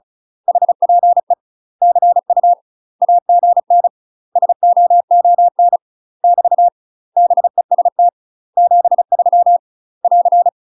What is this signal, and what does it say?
Morse code, heard as a keyed tone, about 34 wpm, 700 Hz. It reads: HPE CU AGN SOON <BT> BEST 73 <AR>
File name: UK MASTER QSOs TRAINING / 35WPM / UK MASTER QSOs 0005